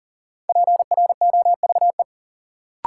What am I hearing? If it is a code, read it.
Morse code, heard as a keyed tone, about 40 wpm, 700 Hz. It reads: PROVE
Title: Sentences from Top 1600 Words - New Word Drill Alt - 40wpm